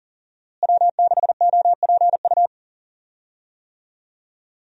Morse code, heard as a keyed tone, about 40 wpm, 700 Hz. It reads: W6OPU
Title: USA Call Signs - 5 Characters - Repeated 2x - 40wpm